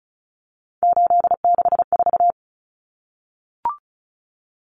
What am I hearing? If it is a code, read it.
Morse code, heard as a keyed tone, about 34 wpm, 700 Hz. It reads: 864
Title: Common Cut Numbers - 3-Digit 35wpm